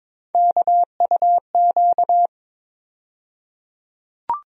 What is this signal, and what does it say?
Morse code, heard as a keyed tone, about 21 wpm, 700 Hz. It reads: KUQ
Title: USA Call Sign Postfixes - 3 Letters 22wpm